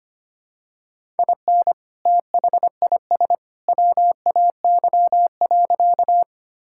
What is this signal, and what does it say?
Morse code, heard as a keyed tone, about 25 wpm, 700 Hz. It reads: IN THIS WAY.
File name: practice / 25wpm / Sentences with 2-Word Phrases - 25wpm